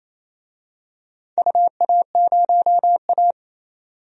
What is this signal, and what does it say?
Morse code, heard as a keyed tone, about 28 wpm, 700 Hz. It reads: UA0A
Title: DXCC-100 Call Signs - 4 Characters - 28wpm